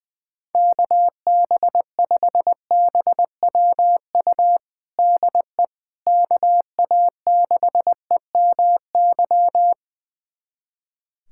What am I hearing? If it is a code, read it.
Morse code, heard as a keyed tone, about 20 wpm, 700 Hz. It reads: KB5BWU DE KA6EMY
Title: USA Callsign DE Callsign - 6 Characters - 20wpm